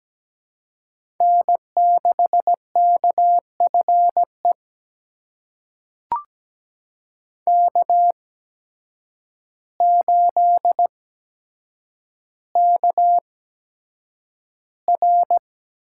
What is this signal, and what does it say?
Morse code, heard as a keyed tone, about 17 wpm, spaced as for 8 wpm, 700 Hz. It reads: N6KFE K8KR
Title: USA Call Signs - 5 Characters - 8x Character Spacing - 17wpm